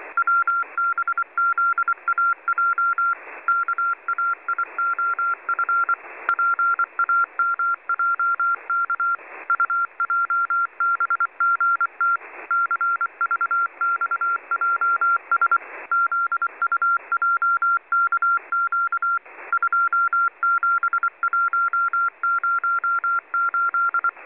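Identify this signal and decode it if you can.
Morse code, heard as a keyed tone, about 24 wpm, 1380 Hz. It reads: WBZAJ KAIOF PAMJK UJBGT CVXJS ZUJKQ 27108
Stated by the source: Morse from numbers stations listening discussion.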